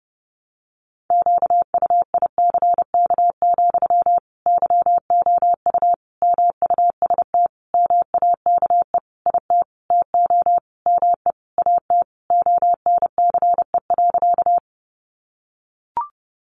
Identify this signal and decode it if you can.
Morse code, heard as a keyed tone, about 30 wpm, 700 Hz. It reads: QUICK, YOU MUST MAKE IT TO ME AT ONCE.
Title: Sentences from Top 500 Words 30wpm